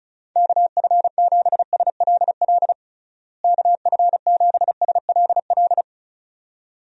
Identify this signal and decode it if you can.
Morse code, heard as a keyed tone, about 36 wpm, 700 Hz. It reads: KF7SLL KF7SLL
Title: USA Call Signs - 6 Characters - Repeated 2x - 35wpm